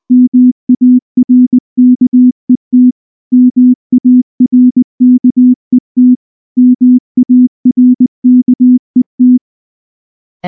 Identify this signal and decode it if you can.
Morse code, heard as a keyed tone, about 19 wpm, 250 Hz. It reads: MARKET MARKET MARKET
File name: morse / morse-training-20WPM-6